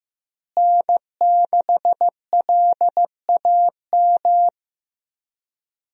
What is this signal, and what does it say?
Morse code, heard as a keyed tone, about 15 wpm, 700 Hz. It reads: N6LAM
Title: USA Call Signs - 5 Characters - Repeated 3x - 15wpm